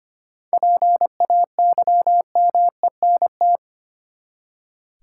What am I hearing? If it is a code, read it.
Morse code, heard as a keyed tone, about 25 wpm, 700 Hz. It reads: PAYMENT
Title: Seven Letter Words - 25wpm